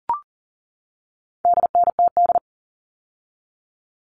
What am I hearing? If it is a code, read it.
Morse code, heard as a keyed tone, about 40 wpm, 700 Hz. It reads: DNTD
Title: All Cut Numbers - 4-Digit 40wpm